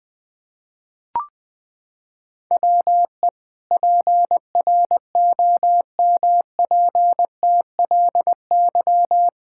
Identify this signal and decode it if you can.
Morse code, heard as a keyed tone, about 19 wpm, 700 Hz. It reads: WE PROMPTLY
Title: Pangram Challenge v2 - 20wpm